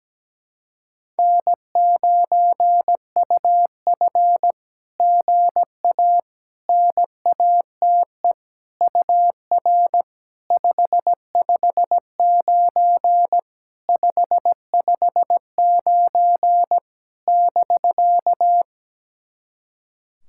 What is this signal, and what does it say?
Morse code, heard as a keyed tone, about 18 wpm, 700 Hz. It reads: N9UF GA NATE UR 559 559 <BK>
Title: SOTA QSO Practice 17wpm